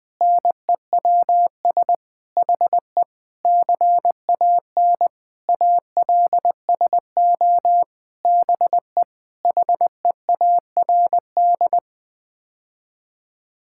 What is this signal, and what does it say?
Morse code, heard as a keyed tone, about 20 wpm, 700 Hz. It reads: NEWS HE CAN ALSO BE HEARD
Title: Sets of 9 Words - 20wpm